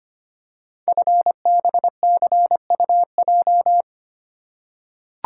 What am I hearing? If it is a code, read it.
Morse code, heard as a keyed tone, about 25 wpm, 700 Hz. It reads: FBCUJ
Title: Code Groups - 5 Letters - 25wpm